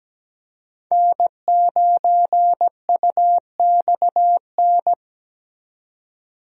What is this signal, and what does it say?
Morse code, heard as a keyed tone, about 17 wpm, 700 Hz. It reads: N9UXN
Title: USA Call Signs - 5 Characters - 17wpm